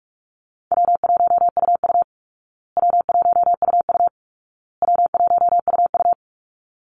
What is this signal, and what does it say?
Morse code, heard as a keyed tone, about 44 wpm, 700 Hz. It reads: W1UU W1UU W1UU
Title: Call Signs Repeated 2x - 45wpm